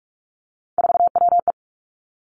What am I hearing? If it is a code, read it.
Morse code, heard as a keyed tone, about 46 wpm, 700 Hz. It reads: 4WE
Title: Three Challenging Letter-Number Combinations 45wpm